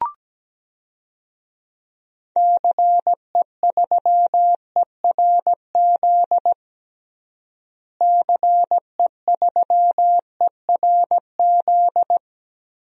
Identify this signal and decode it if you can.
Morse code, heard as a keyed tone, about 17 wpm, 700 Hz. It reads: CE3ERZ CE3ERZ
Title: DXCC-100 Call Signs - 6 Characters - Repeated 3x - 17wpm